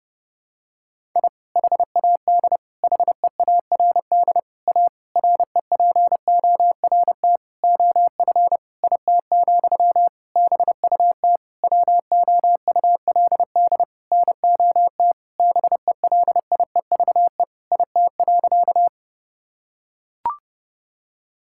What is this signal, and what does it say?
Morse code, heard as a keyed tone, about 28 wpm, 700 Hz. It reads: I HAD HEARD A REPORT OF IT, BUT WOULD NOT BELIEVE IT.
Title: Sentences from Top 1000 Words - Review Part 6 - 30wpm